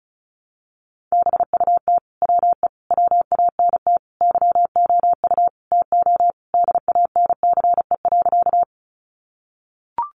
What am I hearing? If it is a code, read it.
Morse code, heard as a keyed tone, about 34 wpm, 700 Hz. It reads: BUT WE WANT YOU TO DANCE.
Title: Sentences from Top 700 Words 35wpm